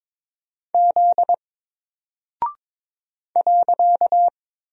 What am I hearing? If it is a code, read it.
Morse code, heard as a keyed tone, about 22 wpm, 700 Hz. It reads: Z .
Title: Single Character (Letter:Number:Punctuation:Prosign) Alt 22wpm